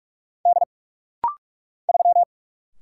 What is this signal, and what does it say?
Morse code, heard as a keyed tone, about 46 wpm, 700 Hz. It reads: D 3
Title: Single Letter-Number Alt - Rapid-Fire 45wpm